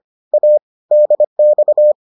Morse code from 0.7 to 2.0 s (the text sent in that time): DX